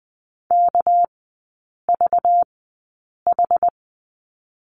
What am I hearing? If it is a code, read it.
Morse code, heard as a keyed tone, about 19 wpm, spaced as for 8 wpm, 690 Hz. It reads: KVH
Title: 198 Lesson 32e - Letter k - Three Characters 20wpm